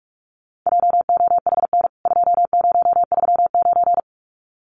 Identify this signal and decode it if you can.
Morse code, heard as a keyed tone, about 44 wpm, 700 Hz. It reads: JOHN 2039
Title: CWT Exchange 45wpm